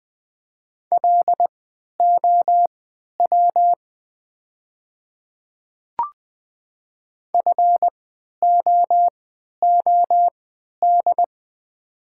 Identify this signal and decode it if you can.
Morse code, heard as a keyed tone, about 20 wpm, spaced as for 11 wpm, 700 Hz. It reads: LOW FOOD
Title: 3x Character Spacing - Top 300 Words - 20wpm